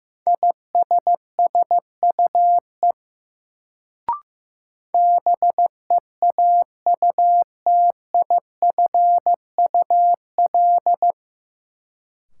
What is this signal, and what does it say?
Morse code, heard as a keyed tone, about 15 wpm, 700 Hz. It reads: ISSUE BEAUTIFUL
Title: Sentences from Top 900 Words - New Word Drill - 15wpm